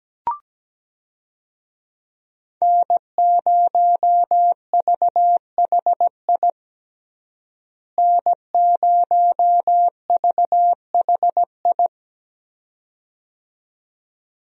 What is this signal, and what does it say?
Morse code, heard as a keyed tone, about 17 wpm, 700 Hz. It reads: N0VHI N0VHI
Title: USA Call Signs - 5 Characters - Repeated 2x - 17wpm